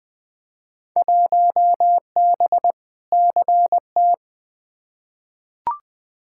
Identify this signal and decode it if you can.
Morse code, heard as a keyed tone, about 20 wpm, 700 Hz. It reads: 1B CT
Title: Field Day Exchanges Encore - 20wpm